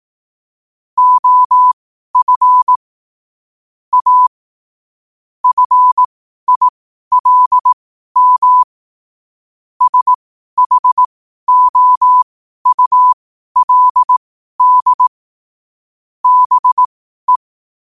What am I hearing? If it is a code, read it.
Morse code, heard as a keyed tone, about 18 wpm, spaced as for 12 wpm, 1000 Hz. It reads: OF A FILM SHOULD BE
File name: qotdfiles / 12wpm